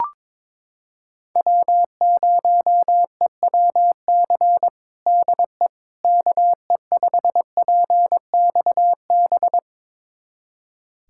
Morse code, heard as a keyed tone, about 22 wpm, 700 Hz. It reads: W0EWC DE KE5PXB
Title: USA Callsign DE Callsign 22wpm